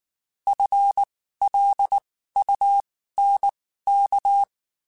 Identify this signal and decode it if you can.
Morse code, heard as a keyed tone, about 19 wpm, spaced as for 14 wpm, 800 Hz. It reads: FLUNK
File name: Spaced / FL Runs sp mp3s / FL Runs sp @18